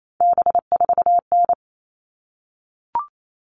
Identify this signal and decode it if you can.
Morse code, heard as a keyed tone, about 28 wpm, 700 Hz. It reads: B4N
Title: All Cut Numbers - 3-Digit 28wpm